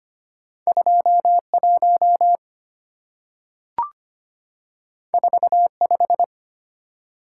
Morse code, heard as a keyed tone, about 25 wpm, 700 Hz. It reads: 21 45
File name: Two Letter-Number Combinations - 25wpm